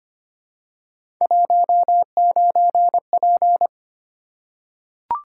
Morse code, heard as a keyed tone, about 25 wpm, 700 Hz. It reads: 19P
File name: Three Letter-Number Combinations - 25wpm